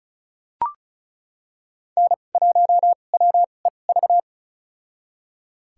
Morse code, heard as a keyed tone, about 36 wpm, spaced as for 25 wpm, 700 Hz. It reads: N1WEV
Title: USA Call Signs - 2x Character Spacing - 35wpm